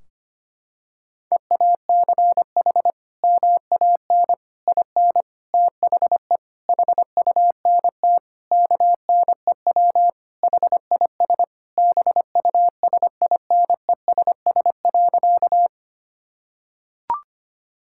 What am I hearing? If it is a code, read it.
Morse code, heard as a keyed tone, about 25 wpm, 700 Hz. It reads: EACH MAN IN THE HUNT KNEW HIS BUSINESS.
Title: Sentences from Top 1600 Words 25wpm